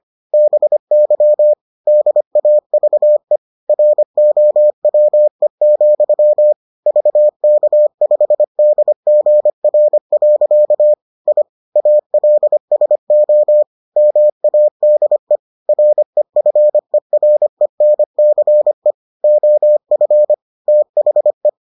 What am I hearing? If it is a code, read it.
Morse code, heard as a keyed tone, about 25 wpm, 600 Hz. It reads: BY DAVE ROWE, VK5DGR. I ALSO MADE REFERENCE OF THE